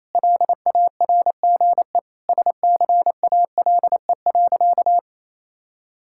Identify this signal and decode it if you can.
Morse code, heard as a keyed tone, about 28 wpm, 700 Hz. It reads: LARGE SCALE.
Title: Sentences from Top 1000 Words - Review Part 5 - 28wpm